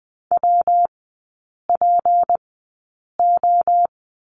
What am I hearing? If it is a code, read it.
Morse code, heard as a keyed tone, about 20 wpm, spaced as for 8 wpm, 700 Hz. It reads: WPO
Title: 152 Lesson 25e - Letter p - Three Characters 20wpm